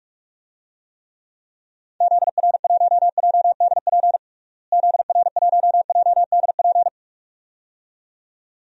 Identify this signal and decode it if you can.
Morse code, heard as a keyed tone, about 41 wpm, 700 Hz. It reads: ZR1JDP ZR1JDP
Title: DXCC-100 Call Signs - 6 Characters - Repeated 2x - 45wpm